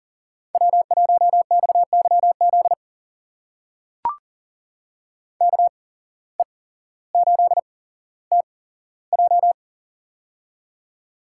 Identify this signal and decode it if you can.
Morse code, heard as a keyed tone, about 40 wpm, spaced as for 17 wpm, 700 Hz. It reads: W1XYZ KE8TJ
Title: USA Call Signs - 5 Characters - 8x Character Spacing - 40wpm